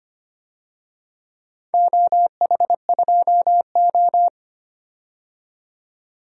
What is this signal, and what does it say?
Morse code, heard as a keyed tone, about 25 wpm, 700 Hz. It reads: OH2O
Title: DXCC-100 Call Signs - 4 Characters - 25wpm